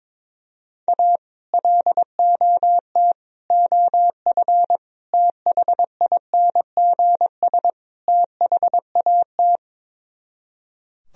Morse code, heard as a keyed tone, about 22 wpm, 700 Hz. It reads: A LOT OF THINGS THAT
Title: Speed-Racing - Sets of 5 Words 22wpm